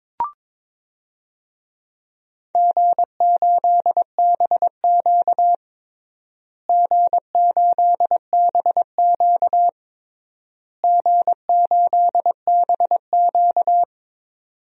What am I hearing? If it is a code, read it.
Morse code, heard as a keyed tone, about 22 wpm, 700 Hz. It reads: G8BQ G8BQ G8BQ
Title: DXCC-100 Call Signs - 4 Characters - Repeated 3x - 22wpm